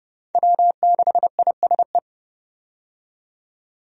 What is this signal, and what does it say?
Morse code, heard as a keyed tone, about 30 wpm, 700 Hz. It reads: W6ISE
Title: USA Call Signs - 5 Characters - 30wpm